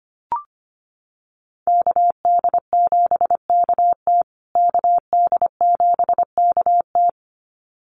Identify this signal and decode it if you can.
Morse code, heard as a keyed tone, about 25 wpm, 700 Hz. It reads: KD7KT KD7KT
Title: USA Call Signs Repeated 2x 25wpm